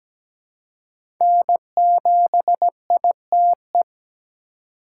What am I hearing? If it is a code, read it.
Morse code, heard as a keyed tone, about 17 wpm, 700 Hz. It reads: N7ITE
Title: USA Call Signs - 5 Characters - 4x Character Spacing - 17wpm